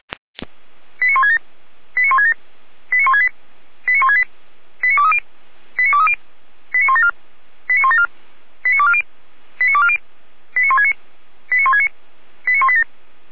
Five-tone selective call (ZVEI), heard as quick runs of five tones, tones of about 70 ms. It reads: 89167 89167 89168 89168 89230 89230 89174 89174 89240 89240 89169 89169 89187